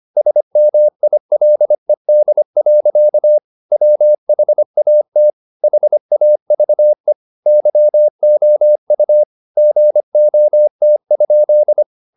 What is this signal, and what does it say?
Morse code, heard as a keyed tone, about 25 wpm, 600 Hz. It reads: SMILED. WHAT HAVE YOU GOT?
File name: the old man and the sea part 1 25wpm